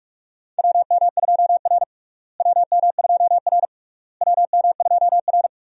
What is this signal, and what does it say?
Morse code, heard as a keyed tone, about 46 wpm, 700 Hz. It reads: WM2R WM2R WM2R
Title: USA Call Signs - 4 Characters - Repeated 3x - 45wpm